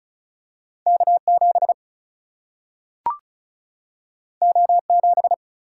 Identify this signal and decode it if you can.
Morse code, heard as a keyed tone, about 36 wpm, 700 Hz. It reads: K7 O7
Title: Single-Letter Single-Number 35wpm